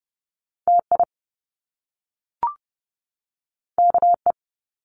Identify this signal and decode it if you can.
Morse code, heard as a keyed tone, about 30 wpm, 700 Hz. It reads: TI KE
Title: Top Two Letter Combinations in Words 30wpm